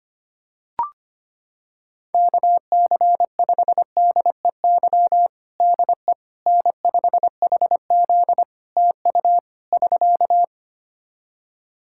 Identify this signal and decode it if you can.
Morse code, heard as a keyed tone, about 25 wpm, 700 Hz. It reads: KC5DEY DE N5HZ TU <SK>
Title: Practice QSO - Exchange 4 - Ending - 25wpm